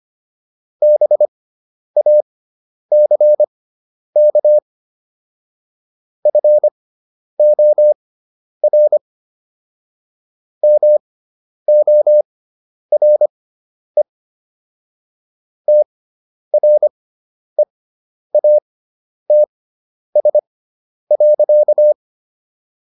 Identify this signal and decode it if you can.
Morse code, heard as a keyed tone, about 25 wpm, spaced as for 10 wpm, 600 Hz. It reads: BACK FOR MORE TREATS.